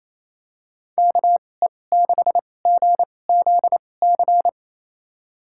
Morse code, heard as a keyed tone, about 28 wpm, spaced as for 20 wpm, 700 Hz. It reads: KE6GZC
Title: USA Call Signs - 6 Characters - 2x Character Spacing - 28wpm